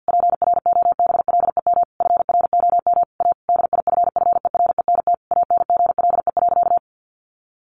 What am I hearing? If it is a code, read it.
Morse code, heard as a keyed tone, about 50 wpm, 700 Hz. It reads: PROBLEM FROM A DIFFERENT ANGLE.
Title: Sentences from Top 1500 Words 50wpm